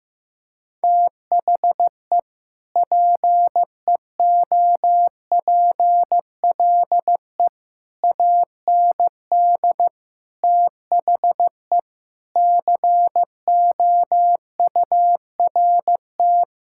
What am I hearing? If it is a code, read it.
Morse code, heard as a keyed tone, about 15 wpm, 700 Hz. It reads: THE PEOPLE AND THE COURT
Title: Sentences from Top 1600 Words 15wpm